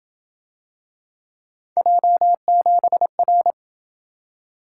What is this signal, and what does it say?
Morse code, heard as a keyed tone, about 28 wpm, 700 Hz. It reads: J7R